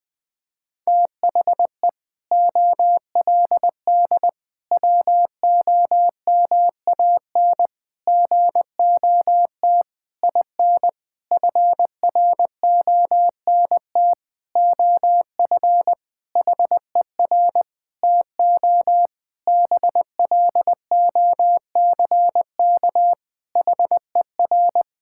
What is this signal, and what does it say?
Morse code, heard as a keyed tone, about 20 wpm, 700 Hz. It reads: THE OLD WOMAN GOT IN FRONT OF HER TO BLOCK HER
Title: Sentences from Top 1600 Words 20wpm